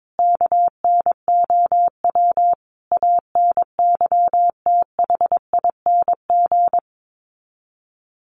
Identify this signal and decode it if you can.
Morse code, heard as a keyed tone, about 22 wpm, 700 Hz. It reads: KNOW ANYTHING
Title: Sets of 5 Words Encore 22wpm